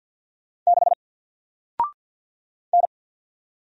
Morse code, heard as a keyed tone, about 50 wpm, 700 Hz. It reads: 6 N
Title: Single Character (Letter:Number:Punctuation:Prosign) 50wpm